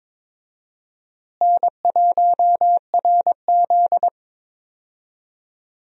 Morse code, heard as a keyed tone, about 22 wpm, 700 Hz. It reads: N1RZ
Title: USA Call Signs - 4 Characters - Repeated 3x - 22wpm